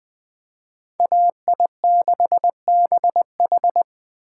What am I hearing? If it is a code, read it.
Morse code, heard as a keyed tone, about 21 wpm, 700 Hz. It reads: AI6BH
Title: Speed Racing - USA 2x2 Call Signs 17wpm